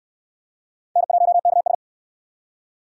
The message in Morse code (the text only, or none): N1KS